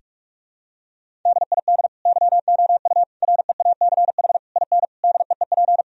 Morse code, heard as a keyed tone, about 46 wpm, 700 Hz. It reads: DID YOU REACH IN DEEP
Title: Sentences from Top 1000 Words - Review Part 3 - 45wpm